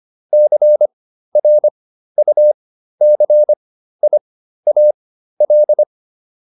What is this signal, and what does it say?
Morse code, heard as a keyed tone, about 25 wpm, spaced as for 13 wpm, 600 Hz. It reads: CRUCIAL